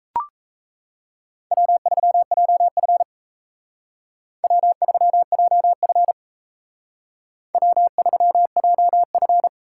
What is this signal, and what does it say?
Morse code, heard as a keyed tone, about 36 wpm, 700 Hz. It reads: W3JF W3JF W3JF